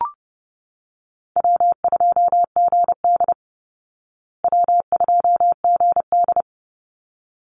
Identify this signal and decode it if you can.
Morse code, heard as a keyed tone, about 30 wpm, 700 Hz. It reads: W2GD W2GD
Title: Easy Call Signs Repeated 2x - 30wpm